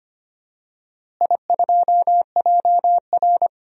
Morse code, heard as a keyed tone, about 25 wpm, 700 Hz. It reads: I2JR
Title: DXCC-100 Call Signs - 4 Characters - Repeated 2x - 25wpm